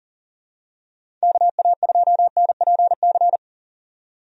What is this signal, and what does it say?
Morse code, heard as a keyed tone, about 38 wpm, 700 Hz. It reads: KA2NPC
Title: USA Call Signs - 6 Characters - 40wpm